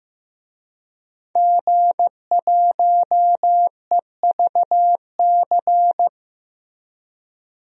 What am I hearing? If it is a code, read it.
Morse code, heard as a keyed tone, about 15 wpm, 700 Hz. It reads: G1EVC
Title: DXCC-100 Call Signs - 5 Characters - Repeated 3x - 15wpm